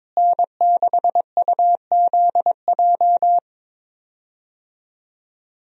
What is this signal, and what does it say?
Morse code, heard as a keyed tone, about 22 wpm, 700 Hz. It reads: N6UZJ
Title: USA Call Signs - 5 Characters - Repeated 3x - 22wpm